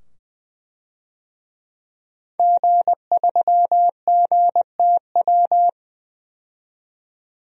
Morse code, heard as a keyed tone, about 21 wpm, 700 Hz. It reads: G3GTW
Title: DXCC-100 Call Signs - 5 Characters - 20wpm